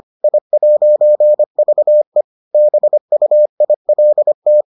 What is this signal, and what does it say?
Morse code, heard as a keyed tone, about 25 wpm, 600 Hz. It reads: I'VE BUILT